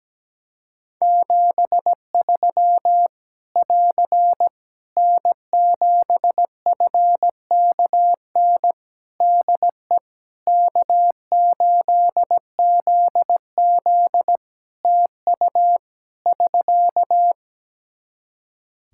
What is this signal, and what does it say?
Morse code, heard as a keyed tone, about 17 wpm, 700 Hz. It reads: 73 <AR> N7FKN DE K8ZZ TU <SK>
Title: Practice QSO - Exchange 4 - Ending - 17wpm